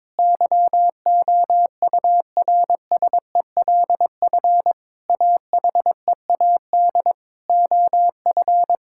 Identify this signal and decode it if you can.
Morse code, heard as a keyed tone, about 22 wpm, 700 Hz. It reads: YOURSELF AHEAD OF
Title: Sentences from Top 1000 Words - Review Part 6 - 22wpm